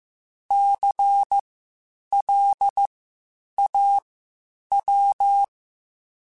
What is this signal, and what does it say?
Morse code, heard as a keyed tone, about 15 wpm, spaced as for 8 wpm, 800 Hz. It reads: CLAW
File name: Spaced Alphas / Spaced Alphas mp3s / Cs sp @13